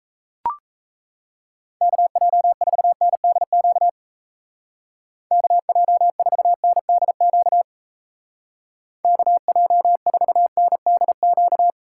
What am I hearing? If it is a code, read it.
Morse code, heard as a keyed tone, about 34 wpm, 700 Hz. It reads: KJ4NDQ KJ4NDQ KJ4NDQ
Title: Speed Racing - USA Call Signs 28wpm